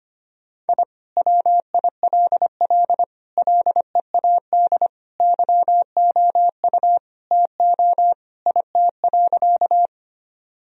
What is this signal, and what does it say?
Morse code, heard as a keyed tone, about 25 wpm, 700 Hz. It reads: I WILL LEAD YOU TO IT.
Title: Sentences from Top 1000 Words - Review Part 2 - 25wpm